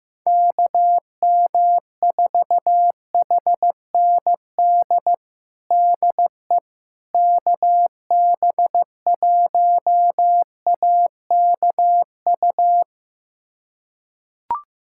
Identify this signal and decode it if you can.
Morse code, heard as a keyed tone, about 16 wpm, 700 Hz. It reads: KM4HND DE KB1AKU
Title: USA Callsign DE Callsign 15wpm